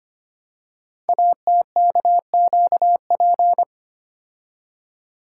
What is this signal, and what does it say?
Morse code, heard as a keyed tone, about 25 wpm, 700 Hz. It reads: ATKQP
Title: Code Groups - 5 Letters - 25wpm